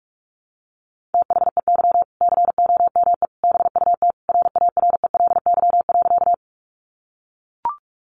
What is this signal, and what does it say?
Morse code, heard as a keyed tone, about 44 wpm, 700 Hz. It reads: THEY COME BUT RARELY.
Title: Sentences from Top 1400 Words 45wpm